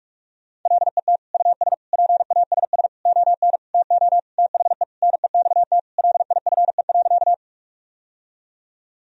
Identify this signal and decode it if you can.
Morse code, heard as a keyed tone, about 42 wpm, 700 Hz. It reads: LET US PASS ON TO THE NEXT LIFE.
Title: Sentences from Top 1000 Words - Review Part 1 - 45wpm